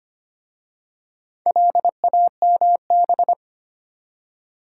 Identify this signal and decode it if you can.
Morse code, heard as a keyed tone, about 25 wpm, 690 Hz. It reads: LAMB
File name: practice / 25wpm / Easy Words with Hints - 25wpm